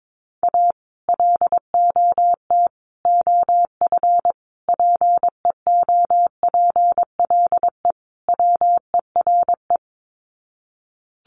Morse code, heard as a keyed tone, about 22 wpm, 700 Hz. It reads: A LOT OF PEOPLE WERE
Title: Sets of 5 Words Encore2 22wpm